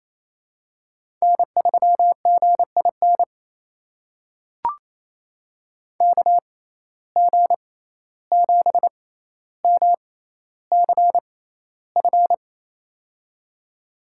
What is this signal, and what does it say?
Morse code, heard as a keyed tone, about 28 wpm, spaced as for 14 wpm, 700 Hz. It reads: N3GIN KG7MCF
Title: USA Call Signs - 6x Character Spacing - 28wpm